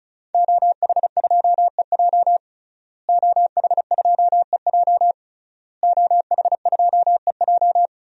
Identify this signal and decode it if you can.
Morse code, heard as a keyed tone, about 36 wpm, 700 Hz. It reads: OH2EJ OH2EJ OH2EJ
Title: DXCC-100 Call Signs - 5 Characters - Repeated 3x - 35wpm